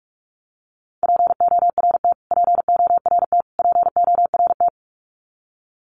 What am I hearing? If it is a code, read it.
Morse code, heard as a keyed tone, about 44 wpm, 700 Hz. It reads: PORT PORT PORT
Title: Top 500 Words Repeated 45wpm